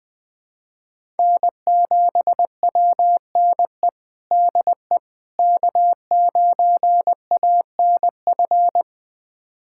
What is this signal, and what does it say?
Morse code, heard as a keyed tone, about 20 wpm, 700 Hz. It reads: N7WNE DE K9ANF
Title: USA Callsign DE Callsign - 5 Characters - 20wpm